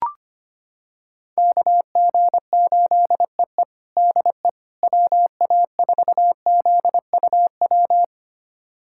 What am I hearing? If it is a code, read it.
Morse code, heard as a keyed tone, about 25 wpm, 700 Hz. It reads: KG8EE DE WA4ZUW
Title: USA Callsign DE Callsign 25wpm